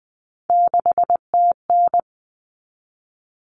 Morse code, heard as a keyed tone, about 20 wpm, 700 Hz. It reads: 6TN
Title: Common Cut Numbers - 3-Digit 20wpm